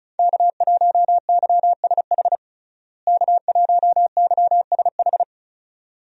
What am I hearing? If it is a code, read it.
Morse code, heard as a keyed tone, about 34 wpm, 700 Hz. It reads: K1YSH K1YSH
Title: USA Call Signs - 5 Characters - Repeated 2x - 35wpm